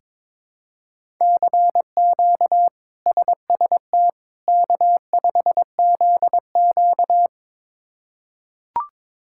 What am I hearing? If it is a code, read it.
Morse code, heard as a keyed tone, about 22 wpm, 700 Hz. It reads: CQ SST K5ZQ
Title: K1USN SST QSO Practice - 22wpm